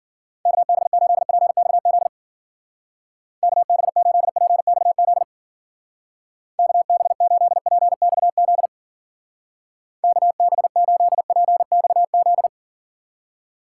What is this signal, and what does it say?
Morse code, heard as a keyed tone, about 51 wpm, 700 Hz. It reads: KB8PXZ KB8PXZ KB8PXZ KB8PXZ